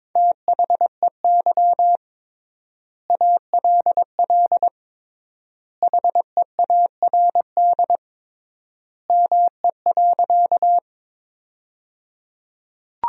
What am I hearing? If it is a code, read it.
Morse code, heard as a keyed tone, about 22 wpm, 700 Hz. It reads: THEY ALL HEARD ME.
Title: 3x Word-Spacing - Sentences from Top 400 Words - 22wpm